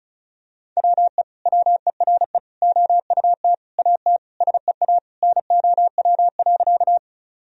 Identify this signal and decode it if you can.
Morse code, heard as a keyed tone, about 34 wpm, 700 Hz. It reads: WE WERE OUT AT SEA NOW.
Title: Sentences from Top 1000 Words - Review Part 1 - 35wpm